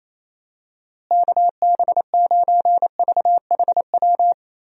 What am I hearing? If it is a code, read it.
Morse code, heard as a keyed tone, about 27 wpm, 700 Hz. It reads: KB9VHW
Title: USA Call Signs - 6 Characters - Repeated 2x - 28wpm